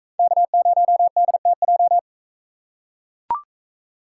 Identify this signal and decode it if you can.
Morse code, heard as a keyed tone, about 41 wpm, 700 Hz. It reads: K0DTJ